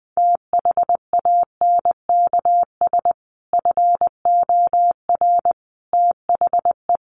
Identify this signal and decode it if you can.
Morse code, heard as a keyed tone, about 20 wpm, 700 Hz. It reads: THANKS FOR THE
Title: Sentences from Top 1500 Words 20wpm